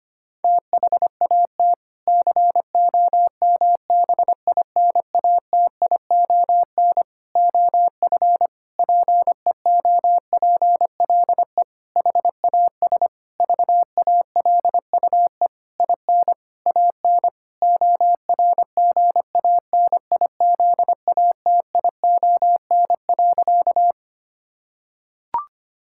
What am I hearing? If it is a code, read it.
Morse code, heard as a keyed tone, about 25 wpm, 700 Hz. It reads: THAT COMBINATION OF PEOPLE HAS VALUE IN AN ORGANIZATION.
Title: Sentences from Top 1000 Words - Review Part 6 - 25wpm